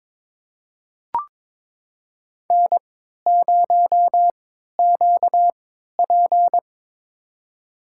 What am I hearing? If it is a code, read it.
Morse code, heard as a keyed tone, about 22 wpm, spaced as for 13 wpm, 700 Hz. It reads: N0QP
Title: USA Call Signs - 4 Characters - 3x Character Spacing - 22wpm